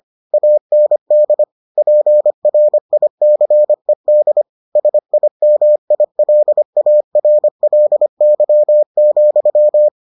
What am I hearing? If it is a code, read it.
Morse code, heard as a keyed tone, about 25 wpm, 600 Hz. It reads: AND PRICED SIMILARLY,